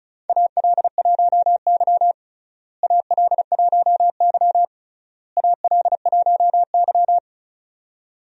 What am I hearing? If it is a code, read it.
Morse code, heard as a keyed tone, about 36 wpm, 700 Hz. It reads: AL1Y AL1Y AL1Y
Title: USA Call Signs - 4 Characters - Repeated 3x - 35wpm